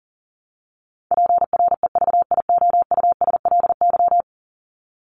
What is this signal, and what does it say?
Morse code, heard as a keyed tone, about 40 wpm, 700 Hz. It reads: PREVIOUSLY
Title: LY Suffix - Top 100 Words with Base Words 40wpm